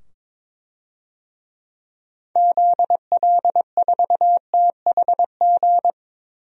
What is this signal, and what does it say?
Morse code, heard as a keyed tone, about 22 wpm, 700 Hz. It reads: ZL4THG
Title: DXCC-100 Call Signs - 6 Characters - 22wpm